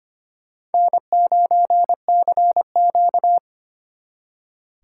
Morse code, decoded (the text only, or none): N9CQ